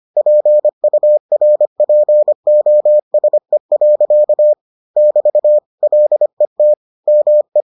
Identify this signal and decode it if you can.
Morse code, heard as a keyed tone, about 25 wpm, 600 Hz. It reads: PURPOSE. <BT> LET ME